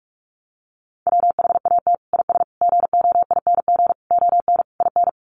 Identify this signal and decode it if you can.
Morse code, heard as a keyed tone, about 44 wpm, 700 Hz. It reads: WHAT IS GOING ON IN
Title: Sets of 5 Words Encore2 45wpm